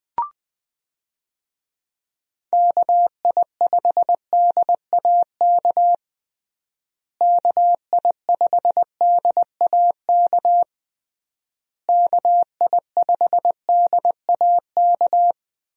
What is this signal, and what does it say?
Morse code, heard as a keyed tone, about 21 wpm, 700 Hz. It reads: KI5DAK KI5DAK KI5DAK
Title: USA Call Signs - 6 Characters - Repeated 3x - 20wpm